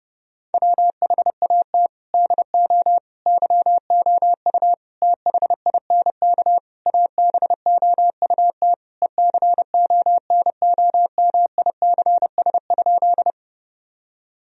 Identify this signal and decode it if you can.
Morse code, heard as a keyed tone, about 32 wpm, 700 Hz. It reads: WHAT DO YOU THINK ABOUT ECONOMICS?
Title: Sentences from Top 1000 Words - Review Part 3 - 30wpm